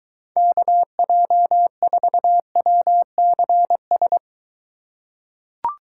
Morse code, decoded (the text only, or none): KJ4WCS